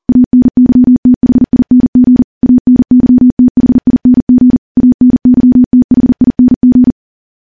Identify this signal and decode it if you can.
Morse code, heard as a keyed tone, about 41 wpm, 250 Hz. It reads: ANYTHING ANYTHING ANYTHING
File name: morse / morse-training-40WPM